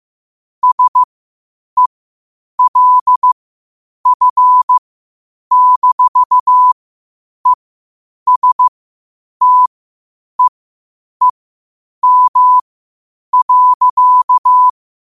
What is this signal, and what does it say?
Morse code, heard as a keyed tone, about 15 wpm, spaced as for 8 wpm, 1000 Hz. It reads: SELF-ESTEEM.
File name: qotdfiles / 8wpm